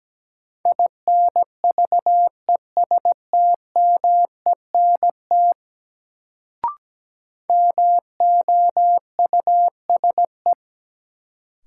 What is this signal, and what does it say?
Morse code, heard as a keyed tone, about 17 wpm, 700 Hz. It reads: INVESTMENT MOUSE
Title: Sentences from Top 700 Words - New Word Drill - 17wpm